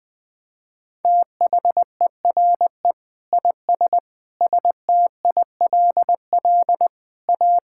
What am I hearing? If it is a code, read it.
Morse code, heard as a keyed tone, about 21 wpm, 700 Hz. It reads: THERE IS STILL A
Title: Sets of 9 Words - 20wpm